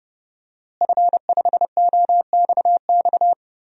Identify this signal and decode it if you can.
Morse code, heard as a keyed tone, about 32 wpm, 700 Hz. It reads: F5OXX
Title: DXCC-100 Call Signs - 5 Characters - Repeated 2x - 30wpm